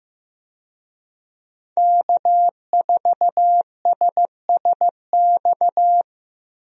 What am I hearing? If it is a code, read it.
Morse code, heard as a keyed tone, about 15 wpm, 700 Hz. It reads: K4SSX
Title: USA Call Signs - 5 Characters - Repeated 3x - 15wpm